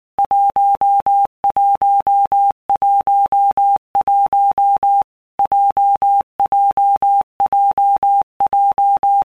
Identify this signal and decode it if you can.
Morse code, heard as a keyed tone, about 19 wpm, 800 Hz. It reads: 1111 JJJJ
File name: All Groups 1-minute Blocks / Groups@18 / JZ17